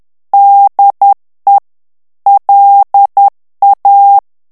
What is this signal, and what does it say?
Morse code, heard as a keyed tone, about 10 wpm, 800 Hz. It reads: DE LA
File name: morse mp3 / FRANCAIS TEXTE CLAIR 2 VITESSE 10 MOTS MINUTES